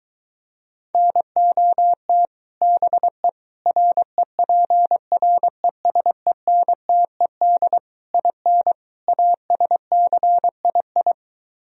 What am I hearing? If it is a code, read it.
Morse code, heard as a keyed tone, about 23 wpm, 700 Hz. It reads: NOT BE REPRESENTED IN ASCII